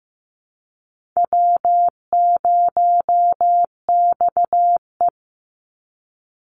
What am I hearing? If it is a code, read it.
Morse code, heard as a keyed tone, about 15 wpm, 700 Hz. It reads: W0XE
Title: Easy Call Signs - 15wpm